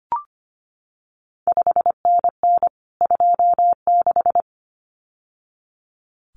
Morse code, heard as a keyed tone, about 25 wpm, 700 Hz. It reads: 5NN 26
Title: CQ WW DX Contest Exchange Encore - 25wpm